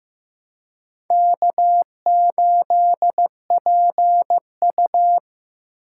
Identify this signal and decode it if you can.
Morse code, heard as a keyed tone, about 15 wpm, 700 Hz. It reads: K8PU
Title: USA Call Signs - 2x Character Spacing - 15wpm